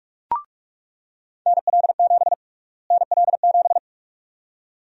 Morse code, heard as a keyed tone, about 46 wpm, 700 Hz. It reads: NL7 NL7
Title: USA Call Sign Prefixes - Repeated 2x - 45wpm